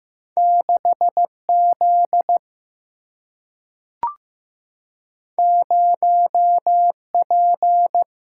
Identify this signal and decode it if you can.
Morse code, heard as a keyed tone, about 15 wpm, 700 Hz. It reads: 6 Z 0 P
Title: Single-Letter Single-Number 15wpm